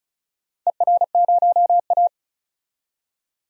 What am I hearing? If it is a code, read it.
Morse code, heard as a keyed tone, about 34 wpm, 700 Hz. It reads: ER0A
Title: DXCC-100 Call Signs - 4 Characters - 35wpm